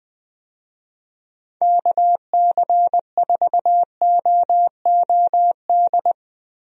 Morse code, heard as a keyed tone, about 20 wpm, 700 Hz. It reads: KC4OOD
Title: USA Call Signs - 6 Characters - Repeated 3x - 20wpm